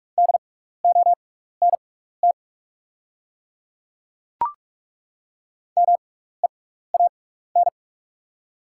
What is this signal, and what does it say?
Morse code, heard as a keyed tone, about 43 wpm, spaced as for 15 wpm, 700 Hz. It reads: DONT MEAN
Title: 6x Character Spacing - Top 300 Words - 45wpm